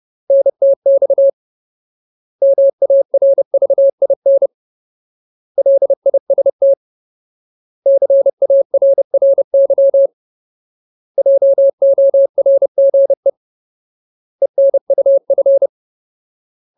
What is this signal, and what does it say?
Morse code, heard as a keyed tone, about 30 wpm, 550 Hz. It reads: NTX MARVIN LIST CARRY JORGE ENUF
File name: Ultimate Word List / 30WPM Ultimate